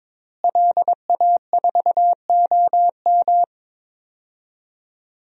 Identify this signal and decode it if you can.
Morse code, heard as a keyed tone, about 22 wpm, 700 Hz. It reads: LA4OM
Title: DXCC-100 Call Signs - 5 Characters - Repeated 2x - 22wpm